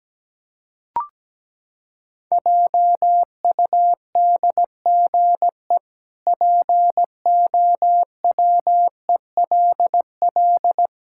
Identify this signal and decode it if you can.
Morse code, heard as a keyed tone, about 17 wpm, 700 Hz. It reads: JUDGE POWELL
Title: Pangram Challenge v2 - 17wpm